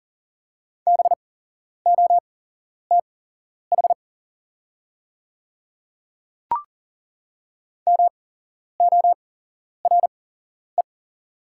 Morse code, heard as a keyed tone, about 40 wpm, spaced as for 11 wpm, 690 Hz. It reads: BOTH MORE
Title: 8x Character Spacing - Top 300 Words - 40wpm